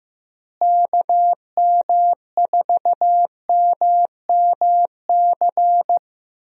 Morse code, heard as a keyed tone, about 16 wpm, 700 Hz. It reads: KM4MMC
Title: USA Call Signs - 6 Characters - Repeated 2x - 15wpm